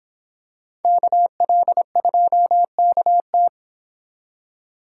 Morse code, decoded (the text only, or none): KL2KT